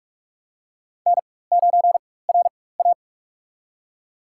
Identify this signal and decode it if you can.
Morse code, heard as a keyed tone, about 41 wpm, spaced as for 20 wpm, 700 Hz. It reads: N9RA
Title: USA Call Signs - 4 Characters - 4x Character Spacing - 45wpm